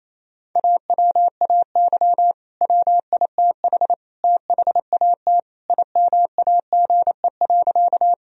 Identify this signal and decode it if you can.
Morse code, heard as a keyed tone, about 28 wpm, 700 Hz. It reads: AWAY WITH THAT IMAGE.
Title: Sentences from Top 1000 Words - Review Part 4 - 28wpm